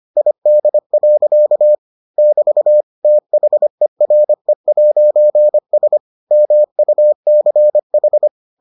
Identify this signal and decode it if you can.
Morse code, heard as a keyed tone, about 25 wpm, 600 Hz. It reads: ID. <BT> THERE'S MUCH